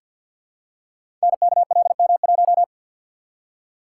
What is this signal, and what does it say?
Morse code, heard as a keyed tone, about 51 wpm, 700 Hz. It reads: NKLM1
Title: Code Groups - 5 Letter:Numbers - 50wpm